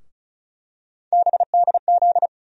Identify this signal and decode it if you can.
Morse code, heard as a keyed tone, about 36 wpm, 700 Hz. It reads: BDZ
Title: USA Call Sign Postfixes - 3 Letters 35wpm